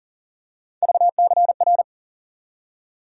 Morse code, heard as a keyed tone, about 40 wpm, 700 Hz. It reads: VCR
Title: USA Call Sign Postfixes - 3 Letters 40wpm